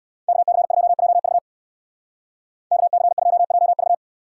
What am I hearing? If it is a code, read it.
Morse code, heard as a keyed tone, about 61 wpm, 700 Hz. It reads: KC2JV KC2JV
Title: Speed Racing - USA 2x2 Call Signs 50wpm